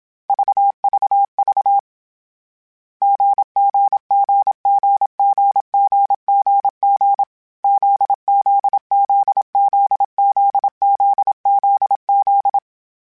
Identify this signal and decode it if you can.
Morse code, heard as a keyed tone, about 27 wpm, 800 Hz. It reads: VVV GGGGGGGG ZZZZZZZZ
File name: Difficult Combos / Difficult Combos-mp3s / GZ GZ7 @25